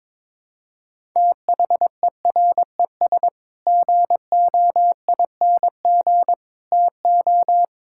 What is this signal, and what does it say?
Morse code, heard as a keyed tone, about 22 wpm, 700 Hz. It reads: THERES GOING TO
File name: practice / 22wpm / Sets of 7 Words - 22wpm